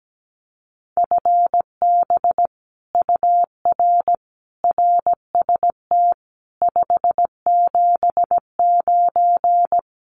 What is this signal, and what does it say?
Morse code, heard as a keyed tone, about 17 wpm, 700 Hz. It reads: FB UR RST 579
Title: Common QSO Phrases Encore 17wpm